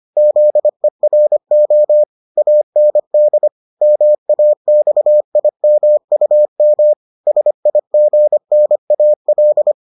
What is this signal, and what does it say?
Morse code, heard as a keyed tone, about 25 wpm, 600 Hz. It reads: ZERO AND MAXIMUM SIGNAL